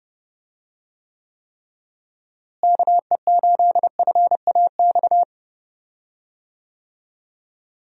Morse code, heard as a keyed tone, about 30 wpm, 700 Hz. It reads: KE8FAX
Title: USA Call Signs - 6 Characters - Repeated 3x - 30wpm